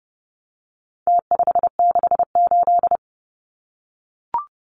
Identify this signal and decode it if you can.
Morse code, heard as a keyed tone, about 30 wpm, 700 Hz. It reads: T568
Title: Numbers - Contest Style - 4-Digit 30wpm